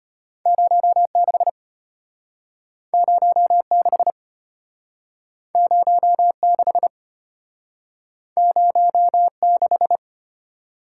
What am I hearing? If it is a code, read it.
Morse code, heard as a keyed tone, about 34 wpm, 700 Hz. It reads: 06 06 06 06